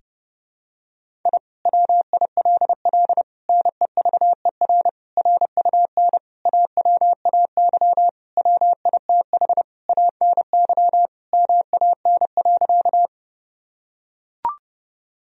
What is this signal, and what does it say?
Morse code, heard as a keyed tone, about 30 wpm, 700 Hz. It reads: I WILL NEVER RUN AWAY WITH ANY MAN.
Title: Sentences from Top 1000 Words - Review Part 1 - 30wpm